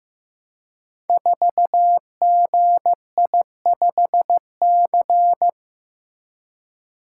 Morse code, heard as a keyed tone, about 15 wpm, 700 Hz. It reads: 4GI5C
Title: Code Groups - 5 Letter:Numbers - 15wpm